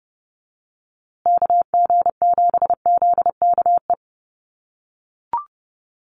Morse code, heard as a keyed tone, about 30 wpm, 700 Hz. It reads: KG7ZKE
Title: USA Call Signs Repeated 3x 30wpm